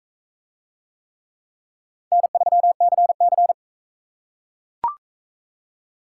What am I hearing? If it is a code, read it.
Morse code, heard as a keyed tone, about 41 wpm, 700 Hz. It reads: N3CC